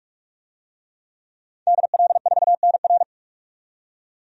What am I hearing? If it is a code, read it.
Morse code, heard as a keyed tone, about 46 wpm, 700 Hz. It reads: DL4NR
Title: DXCC-100 Call Signs - 5 Characters - 45wpm